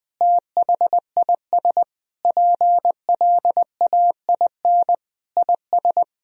Morse code, heard as a keyed tone, about 20 wpm, 700 Hz. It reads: THIS PLAIN IS
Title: Sentences from Top 1000 Words - Review Part 4 - 20wpm